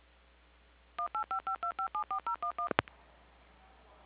DTMF, heard as telephone touch tones, tones of about 110 ms. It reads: 18552574011